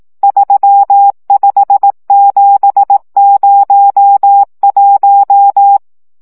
Morse code, heard as a keyed tone, about 18 wpm, 800 Hz. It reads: 35701
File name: morze vizsga 2011-1 15